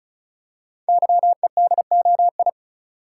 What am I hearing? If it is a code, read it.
Morse code, heard as a keyed tone, about 36 wpm, 700 Hz. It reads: YEDOI
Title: Code Groups - 5 Letters - 35wpm